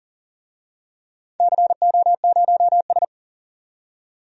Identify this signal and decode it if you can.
Morse code, heard as a keyed tone, about 40 wpm, 700 Hz. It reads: CO0S
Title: DXCC-100 Call Signs - 4 Characters - 40wpm